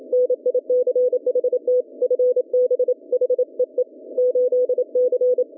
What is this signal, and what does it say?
Morse code, heard as a keyed tone, about 27 wpm, 440 Hz. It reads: NICHT FB HEE 8C